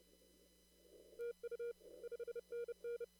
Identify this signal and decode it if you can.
Morse code, heard as a keyed tone, about 30 wpm, 470 Hz. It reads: TU 5NN
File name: CW / kt5j wpx cw-op-k5pi